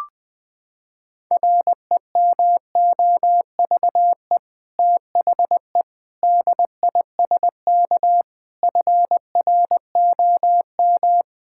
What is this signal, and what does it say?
Morse code, heard as a keyed tone, about 20 wpm, 700 Hz. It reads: REMOVE THE DISK FROM
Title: Sentences from Top 1000 Words - Review Part 6 - 20wpm